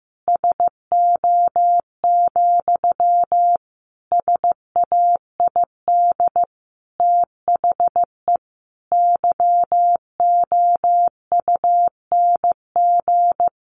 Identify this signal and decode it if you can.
Morse code, heard as a keyed tone, about 15 wpm, 700 Hz. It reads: SO, SAID THE YOUNG
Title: Sentences from Top 900 Words 15wpm